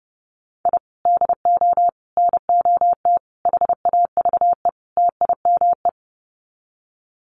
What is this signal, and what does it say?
Morse code, heard as a keyed tone, about 28 wpm, 700 Hz. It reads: I DO NOT HAVE TIME
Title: Sets of 5 Words Encore 30wpm